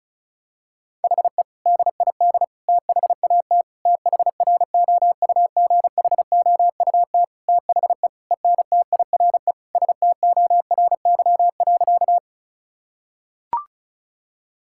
Morse code, heard as a keyed tone, about 36 wpm, 700 Hz. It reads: HE DID THAT THROUGHOUT THE ENTIRE STORY.
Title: Sentences from Top 1000 Words - Review Part 5 - 35wpm